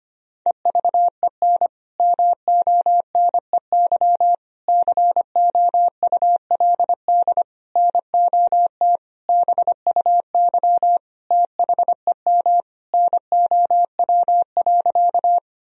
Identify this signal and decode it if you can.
Morse code, heard as a keyed tone, about 25 wpm, 700 Hz. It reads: EVEN MONEY COULD NOT BUY THEM NOW.
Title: Sentences from Top 1000 Words - Review Part 4 - 25wpm